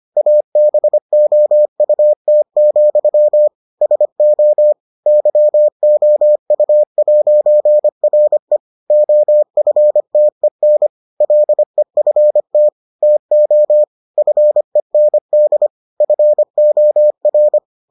Morse code, heard as a keyed tone, about 25 wpm, 600 Hz. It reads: ABOUT, SO YOU'RE OFTEN LEFT TO FEND FOR